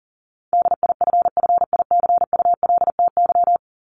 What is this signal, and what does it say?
Morse code, heard as a keyed tone, about 40 wpm, 700 Hz. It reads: DIFFICULTY
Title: Top 1251-1500 Words 40wpm